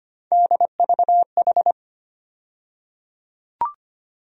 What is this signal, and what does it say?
Morse code, heard as a keyed tone, about 25 wpm, 700 Hz. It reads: DVH
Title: USA Call Sign Postfixes - 3 Letters 25wpm